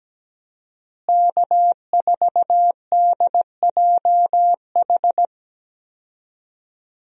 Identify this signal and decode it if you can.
Morse code, heard as a keyed tone, about 17 wpm, 700 Hz. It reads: K4DJH
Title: USA Call Signs - 5 Characters - Repeated 2x - 17wpm